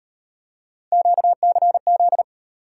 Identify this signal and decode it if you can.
Morse code, heard as a keyed tone, about 38 wpm, 700 Hz. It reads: QCZ